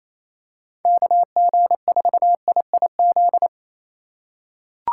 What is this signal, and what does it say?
Morse code, heard as a keyed tone, about 28 wpm, 700 Hz. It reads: KG4IIZ
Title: USA Call Signs - 6 Characters - 4x Character Spacing - 28wpm